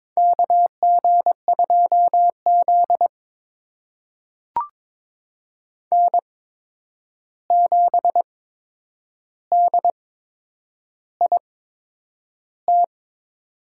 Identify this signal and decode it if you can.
Morse code, heard as a keyed tone, about 22 wpm, spaced as for 9 wpm, 700 Hz. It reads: KG2Z N7DIT
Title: USA Call Signs - 8x Character Spacing - 22wpm